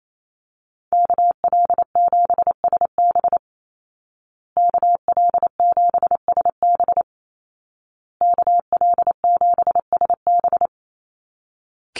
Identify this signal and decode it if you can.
Morse code, heard as a keyed tone, about 28 wpm, 700 Hz. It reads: KL7SB KL7SB KL7SB
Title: Call Signs Repeated 2x - 28wpm